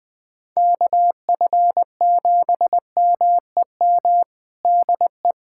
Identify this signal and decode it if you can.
Morse code, heard as a keyed tone, about 21 wpm, 700 Hz. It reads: KF7MEM DE
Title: USA Callsign DE Callsign - 6 Characters - 20wpm